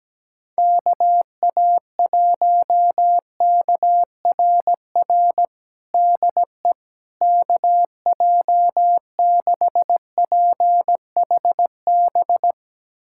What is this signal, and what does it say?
Morse code, heard as a keyed tone, about 17 wpm, 700 Hz. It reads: KA1KRR DE KJ6PHB
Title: USA Callsign DE Callsign - 6 Characters - 17wpm